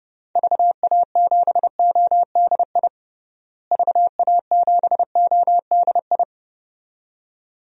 Morse code, heard as a keyed tone, about 29 wpm, 700 Hz. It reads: VA7ODI VA7ODI
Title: DXCC-100 Call Signs - 6 Characters - Repeated 2x - 30wpm